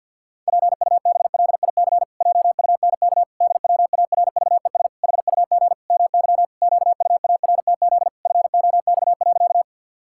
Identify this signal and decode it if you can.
Morse code, heard as a keyed tone, about 51 wpm, 700 Hz. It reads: PUBLIC JUNK DWARVES HUG MY QUARTZ FOX.
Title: Pangram Challenge v2 - 50wpm